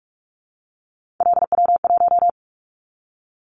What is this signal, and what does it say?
Morse code, heard as a keyed tone, about 46 wpm, 700 Hz. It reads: LW1
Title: Callsign Prefixes for Top 100 DXCC - 45wpm